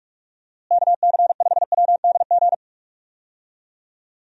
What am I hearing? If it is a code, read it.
Morse code, heard as a keyed tone, about 46 wpm, 700 Hz. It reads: KC5WDG
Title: USA Call Signs - 6 Characters - Repeated 3x - 45wpm